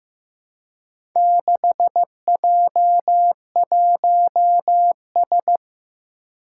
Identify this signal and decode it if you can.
Morse code, heard as a keyed tone, about 15 wpm, 700 Hz. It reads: 6J1S
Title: Four Challenging Letter-Number Combinations 15wpm